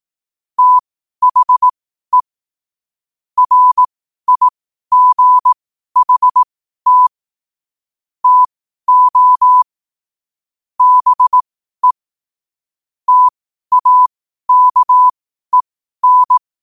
Morse code, heard as a keyed tone, about 18 wpm, spaced as for 12 wpm, 1000 Hz. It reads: THE RIGHT TO BE TAKEN
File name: qotdfiles / 12wpm